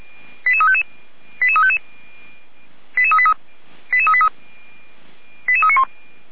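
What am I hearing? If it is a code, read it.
Five-tone selective call (ZVEI), heard as quick runs of five tones, tones of about 70 ms. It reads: 88355 88355 88383 88383 88391
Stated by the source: ZVEI five-tone selective call 08836 received on an SDR.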